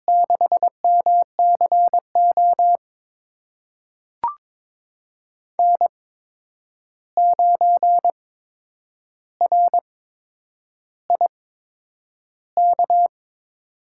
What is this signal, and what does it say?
Morse code, heard as a keyed tone, about 22 wpm, spaced as for 9 wpm, 700 Hz. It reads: 6MCO N9RIK
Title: USA Call Signs - 5 Characters - 8x Character Spacing - 22wpm